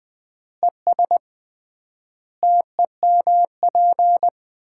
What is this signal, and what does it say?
Morse code, heard as a keyed tone, about 21 wpm, 700 Hz. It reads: ES TEMP
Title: Common QSO Phrases - 3x Word-Spacing - 20wpm